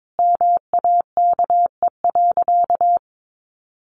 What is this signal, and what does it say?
Morse code, heard as a keyed tone, about 22 wpm, 700 Hz. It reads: MAKE.
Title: Sentences from Top 1100 Words 22wpm